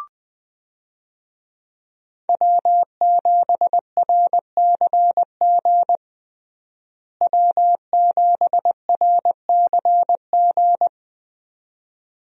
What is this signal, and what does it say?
Morse code, heard as a keyed tone, about 20 wpm, 700 Hz. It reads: W7RCG W7RCG
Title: USA Call Signs - 5 Characters - Repeated 2x - 20wpm